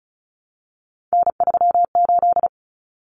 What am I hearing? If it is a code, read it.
Morse code, heard as a keyed tone, about 34 wpm, 700 Hz. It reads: N38
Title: Common Cut Numbers - 3-Digit 35wpm